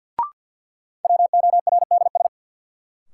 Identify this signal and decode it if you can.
Morse code, heard as a keyed tone, about 53 wpm, 700 Hz. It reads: WORDS
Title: Sentences from Top 1000 Words - Review Part 6 - New Word Drill - 50wpm